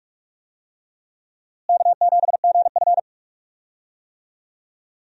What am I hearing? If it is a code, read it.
Morse code, heard as a keyed tone, about 47 wpm, 700 Hz. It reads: K7GF
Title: USA Call Signs - 4 Characters - Repeated 3x - 45wpm